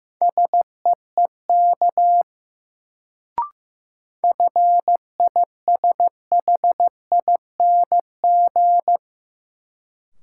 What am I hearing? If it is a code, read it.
Morse code, heard as a keyed tone, about 15 wpm, 700 Hz. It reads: SEEK FISHING
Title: Sentences from Top 1600 Words - New Word Drill Alt - 15wpm